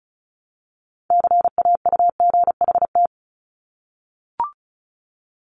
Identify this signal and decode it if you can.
Morse code, heard as a keyed tone, about 36 wpm, 700 Hz. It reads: CAUGHT
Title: Multi-letter Phonemes Repeated 3x 35wpm